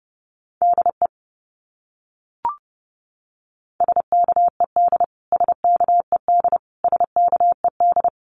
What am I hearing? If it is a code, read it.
Morse code, heard as a keyed tone, about 30 wpm, 700 Hz. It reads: DE SKED SKED SKED
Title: Abbreviations Repeated 3x - 30wpm